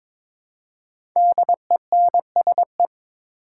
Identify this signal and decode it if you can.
Morse code, heard as a keyed tone, about 22 wpm, 700 Hz. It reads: DENSE
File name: 5 Letter Words - First 3 Letters Given - 22wpm